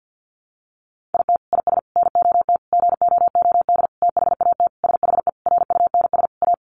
Morse code, heard as a keyed tone, about 50 wpm, 700 Hz. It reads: IT IS NOT GOOD THAT SHE RUNS A
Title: Sentences from Top 600 Words 50wpm